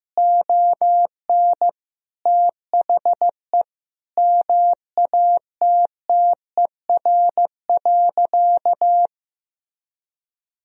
Morse code, heard as a keyed tone, about 15 wpm, 700 Hz. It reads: ON THE MATTER.
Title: Sentences from Top 1000 Words - Review Part 4 - 15wpm